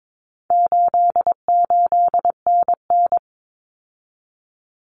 Morse code, heard as a keyed tone, about 22 wpm, 700 Hz. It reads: 88NN
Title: Common Cut Numbers - 4-Digit 22wpm